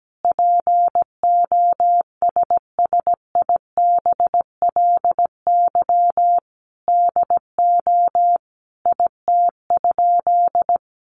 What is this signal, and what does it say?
Morse code, heard as a keyed tone, about 17 wpm, 700 Hz. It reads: POSSIBLY DO IT?
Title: Sentences from Top 1300 Words 17wpm